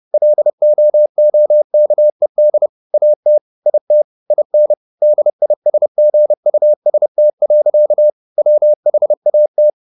Morse code, heard as a keyed tone, about 30 wpm, 600 Hz. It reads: LOOKED AT IT IN DISGUST. WHAT
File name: the old man and the sea / the old man and the sea 1 30wpm